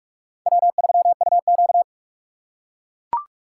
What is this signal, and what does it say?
Morse code, heard as a keyed tone, about 44 wpm, 700 Hz. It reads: W3UQ